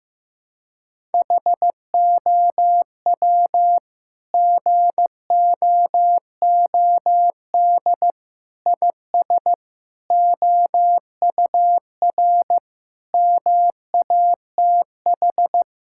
Morse code, heard as a keyed tone, about 15 wpm, 700 Hz. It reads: HOW GOOD IS OUR MATH
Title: Sentences from Top 1000 Words - Review Part 6 - 15wpm